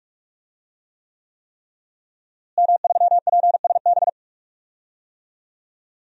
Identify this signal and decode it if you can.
Morse code, heard as a keyed tone, about 42 wpm, 700 Hz. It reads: M3PSB
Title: DXCC-100 Call Signs - 5 Characters - Repeated 3x - 45wpm